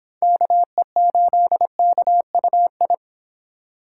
Morse code, heard as a keyed tone, about 26 wpm, 700 Hz. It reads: KE8KUI